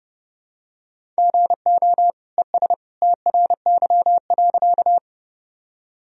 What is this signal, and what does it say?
Morse code, heard as a keyed tone, about 31 wpm, 700 Hz. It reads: GO ES TRY.
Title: Sentences from Top 200 Words and Abbreviations 30wpm